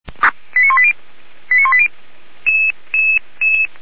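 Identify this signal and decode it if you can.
Five-tone selective call (ZVEI), heard as quick runs of five tones, tones of about 70 ms. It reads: 89180 89180
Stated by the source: ZVEI five-tone call received on an SDR, fire brigade.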